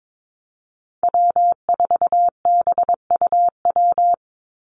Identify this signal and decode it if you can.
Morse code, heard as a keyed tone, about 22 wpm, 700 Hz. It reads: W4BUW
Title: Call Signs Repeated 2x - 22wpm